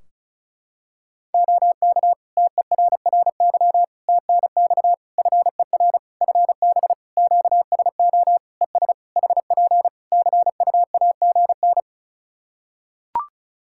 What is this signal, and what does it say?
Morse code, heard as a keyed tone, about 36 wpm, 700 Hz. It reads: OK TERRY TNX FER FB QSO ES HP CUAGN
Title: Practice QSO - Exchange 4 - Ending - 35wpm